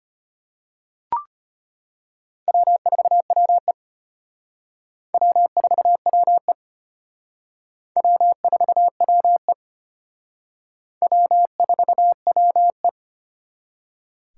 Morse code, heard as a keyed tone, about 33 wpm, 700 Hz. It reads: W4WE W4WE W4WE W4WE